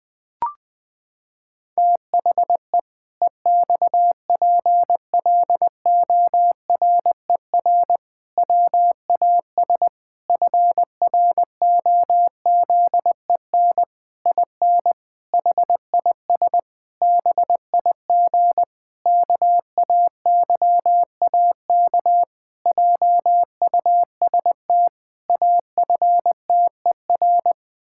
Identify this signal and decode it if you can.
Morse code, heard as a keyed tone, about 20 wpm, 700 Hz. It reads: THE EXPLORER WAS FROZEN IN HIS BIG KAYAK JUST AFTER
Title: Pangram Challenge v2 - 20wpm